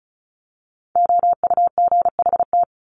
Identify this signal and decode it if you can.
Morse code, heard as a keyed tone, about 34 wpm, 700 Hz. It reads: OUGHT
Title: Top 1751-2000 Words 35wpm